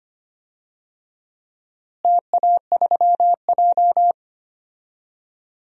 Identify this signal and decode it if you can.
Morse code, heard as a keyed tone, about 25 wpm, 700 Hz. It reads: TA3J
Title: DXCC-100 Call Signs - 4 Characters - Repeated 3x - 25wpm